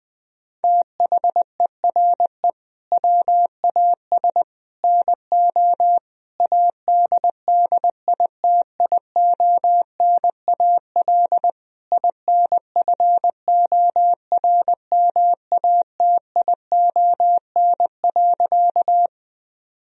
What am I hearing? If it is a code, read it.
Morse code, heard as a keyed tone, about 20 wpm, 700 Hz. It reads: THERE WAS NO ADDITIONAL INFORMATION.
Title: Sentences from Top 1000 Words - Review Part 4 - 20wpm